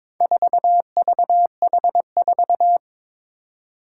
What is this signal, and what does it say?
Morse code, heard as a keyed tone, about 21 wpm, 700 Hz. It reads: 4VH4
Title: Four Challenging Letter-Number Combinations 22wpm